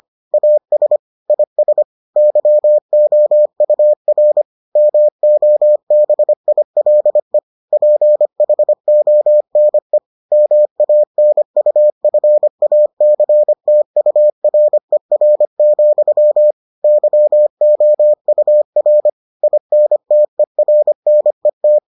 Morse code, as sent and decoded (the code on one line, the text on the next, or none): .- ... / .. ... / -.-- --- ..- .-. / -- --- -... .. .-.. / .--. .... --- -. / -- .- -. ..- ..-. .- -.-. - ..- .-. .-. --..-- / -.-- --- ..- .-. / .. -. - . .-. -. -
AS IS YOUR MOBILE PHONE MANUFACTURER, YOUR INTERNET